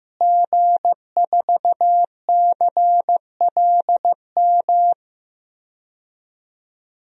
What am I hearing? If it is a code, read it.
Morse code, heard as a keyed tone, about 15 wpm, 700 Hz. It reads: G4CLM
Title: DXCC-100 Call Signs - 5 Characters - Repeated 3x - 15wpm